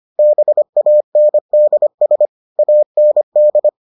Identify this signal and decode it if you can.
Morse code, heard as a keyed tone, about 25 wpm, 600 Hz. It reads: BANDS AND